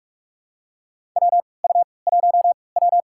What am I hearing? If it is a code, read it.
Morse code, heard as a keyed tone, about 46 wpm, spaced as for 25 wpm, 700 Hz. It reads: WU1W
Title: USA Call Signs - 4 Characters - 3x Character Spacing - 45wpm